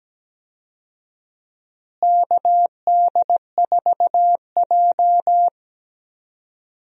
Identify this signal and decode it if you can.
Morse code, heard as a keyed tone, about 17 wpm, 700 Hz. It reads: KD4J
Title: USA Call Signs - 4 Characters - Repeated 2x - 17wpm